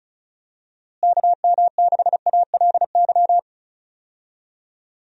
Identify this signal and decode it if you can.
Morse code, heard as a keyed tone, about 36 wpm, 700 Hz. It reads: KM6ALY
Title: USA Call Signs - 6 Characters - Repeated 2x - 35wpm